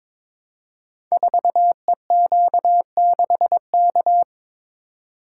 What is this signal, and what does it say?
Morse code, heard as a keyed tone, about 22 wpm, 700 Hz. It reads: 4EQ6K
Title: Code Groups - 5 Letter:Numbers - 22wpm